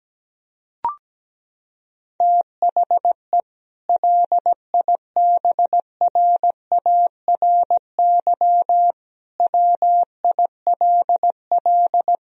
Morse code, heard as a keyed tone, about 17 wpm, 700 Hz. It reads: THE LIBRARY WILL
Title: Sentences from Top 1000 Words - Review Part 4 - 17wpm